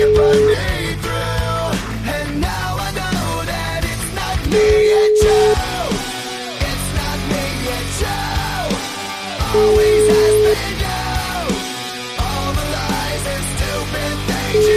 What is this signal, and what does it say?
Ringback tone (Europe, 425 Hz), with noise.